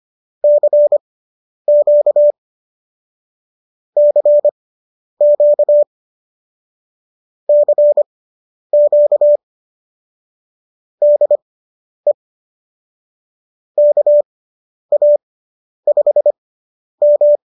Morse code, heard as a keyed tone, about 25 wpm, spaced as for 10 wpm, 590 Hz. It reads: CQ CQ CQ DE KA5M